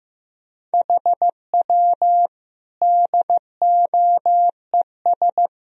Morse code, heard as a keyed tone, about 15 wpm, 700 Hz. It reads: HW DOES
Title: Sentences from Top 200 Words and Abbreviations 15wpm